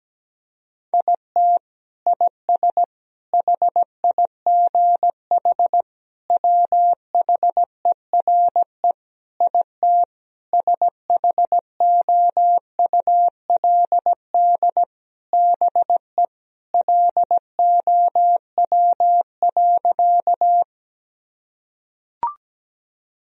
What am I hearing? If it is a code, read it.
Morse code, heard as a keyed tone, about 17 wpm, 700 Hz. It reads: IT IS HIGH WHERE IT SHOULD BE LOW.
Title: Sentences from Top 1000 Words - Review Part 1 - 17wpm